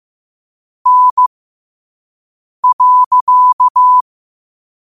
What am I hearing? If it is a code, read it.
Morse code, heard as a keyed tone, about 15 wpm, spaced as for 5 wpm, 1000 Hz. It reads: N.
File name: qotdfiles / 5wpm